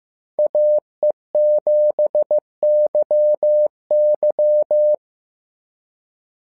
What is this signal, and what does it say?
Morse code, heard as a keyed tone, about 15 wpm, 600 Hz. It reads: AE7YY
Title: SOTA QSO Practice 15wpm